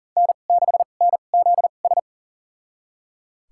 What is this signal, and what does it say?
Morse code, heard as a keyed tone, about 40 wpm, spaced as for 29 wpm, 700 Hz. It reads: N6NZS
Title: USA Call Signs - 5 Characters - 2x Character Spacing - 40wpm